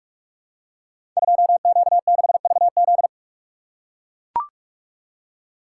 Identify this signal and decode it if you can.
Morse code, heard as a keyed tone, about 43 wpm, 700 Hz. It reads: 2Q6VZ
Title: Code Groups - 5 Letter:Numbers - 45wpm